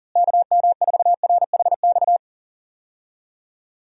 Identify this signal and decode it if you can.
Morse code, heard as a keyed tone, about 40 wpm, 700 Hz. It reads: KM4RHX
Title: USA Call Signs - 6 Characters - Repeated 3x - 40wpm